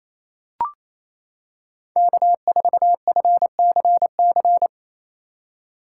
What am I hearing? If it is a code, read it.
Morse code, heard as a keyed tone, about 28 wpm, 700 Hz. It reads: K4FCC
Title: K1USN SST QSO Practice - 28wpm